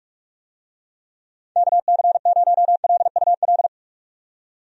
Morse code, heard as a keyed tone, about 42 wpm, 700 Hz. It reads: KC0LUL
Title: USA Call Signs - 6 Characters - 45wpm